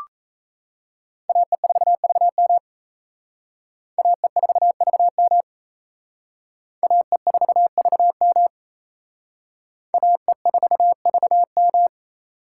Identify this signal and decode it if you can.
Morse code, heard as a keyed tone, about 34 wpm, 700 Hz. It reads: AE4VM AE4VM AE4VM AE4VM
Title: Speed Racing - USA 2x2 Call Signs 28wpm